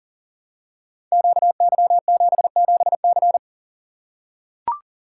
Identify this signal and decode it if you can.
Morse code, heard as a keyed tone, about 40 wpm, 700 Hz. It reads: QY77C
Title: Code Groups - 5 Letter:Numbers - 40wpm